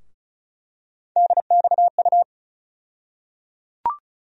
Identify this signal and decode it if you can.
Morse code, heard as a keyed tone, about 36 wpm, 700 Hz. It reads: DXU
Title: USA Call Sign Postfixes - 3 Letters 35wpm